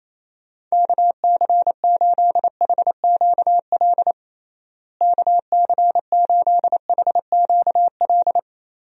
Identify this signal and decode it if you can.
Morse code, heard as a keyed tone, about 28 wpm, 700 Hz. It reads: KC8HQL KC8HQL
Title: USA Call Signs - 6 Characters - Repeated 2x - 28wpm